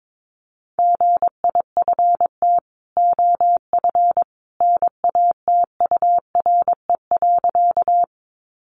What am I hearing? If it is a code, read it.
Morse code, heard as a keyed tone, about 22 wpm, 700 Hz. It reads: GIFT OF NATURE.
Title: Sentences from Top 900 Words 22wpm